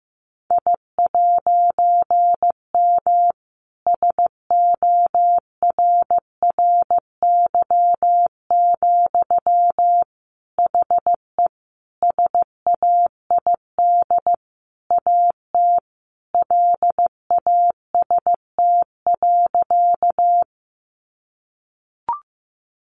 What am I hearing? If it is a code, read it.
Morse code, heard as a keyed tone, about 15 wpm, 700 Hz. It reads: I'M SORRY, HE SAID AT LAST.
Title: A Sound Of Thunder 15wpm